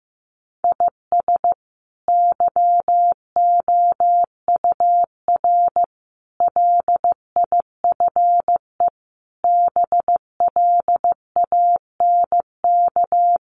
Morse code, heard as a keyed tone, about 15 wpm, 700 Hz. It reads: IS YOUR LIFE BLANK
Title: Sentences from Top 1500 Words 15wpm